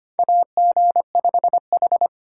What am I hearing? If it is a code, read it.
Morse code, heard as a keyed tone, about 25 wpm, 700 Hz. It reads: AG5H
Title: USA Call Signs - 4 Characters - 8x Character Spacing - 25wpm